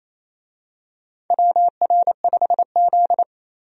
Morse code, heard as a keyed tone, about 28 wpm, 700 Hz. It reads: WR5Z
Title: USA Call Signs - 4 Characters - 28wpm